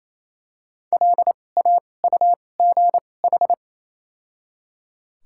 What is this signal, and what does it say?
Morse code, heard as a keyed tone, about 28 wpm, spaced as for 20 wpm, 700 Hz. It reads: LAUGH
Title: 2x Character Spacing - Top 500 Words - 28wpm